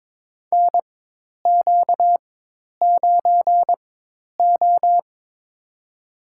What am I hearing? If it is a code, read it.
Morse code, heard as a keyed tone, about 22 wpm, spaced as for 10 wpm, 700 Hz. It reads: NQ9O
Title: USA Call Signs - 4 Characters - 4x Character Spacing - 22wpm